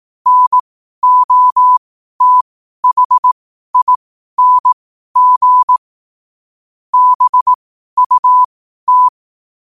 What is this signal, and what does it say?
Morse code, heard as a keyed tone, about 18 wpm, spaced as for 12 wpm, 1000 Hz. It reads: NOTHING BUT
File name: qotdfiles / 12wpm